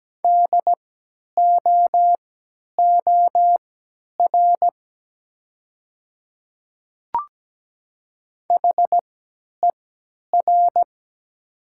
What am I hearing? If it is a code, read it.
Morse code, heard as a keyed tone, about 17 wpm, spaced as for 10 wpm, 700 Hz. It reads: DOOR HER
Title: 3x Character Spacing - Top 300 Words - 17wpm